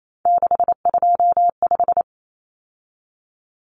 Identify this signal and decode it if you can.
Morse code, heard as a keyed tone, about 28 wpm, 700 Hz. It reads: 625
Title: Numbers - 3-Digit 28wpm